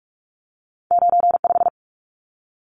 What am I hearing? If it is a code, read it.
Morse code, heard as a keyed tone, about 46 wpm, 700 Hz. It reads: 95
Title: Numbers - 2-Digit 45wpm